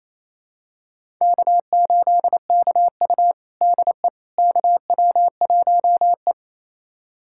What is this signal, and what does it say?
Morse code, heard as a keyed tone, about 28 wpm, 700 Hz. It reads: K8KU DE KW1E
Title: USA Callsign DE Callsign - 4 Characters - 28wpm